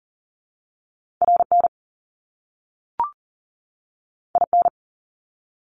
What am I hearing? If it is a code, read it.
Morse code, heard as a keyed tone, about 38 wpm, 700 Hz. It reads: RN IN
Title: Top Two Letter Combinations in Words 40wpm